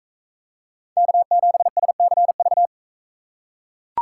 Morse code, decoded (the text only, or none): K7SCV